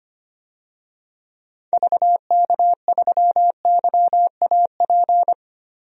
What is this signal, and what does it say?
Morse code, heard as a keyed tone, about 25 wpm, 700 Hz. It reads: VK3YAP
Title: DXCC-100 Call Signs - 6 Characters - 25wpm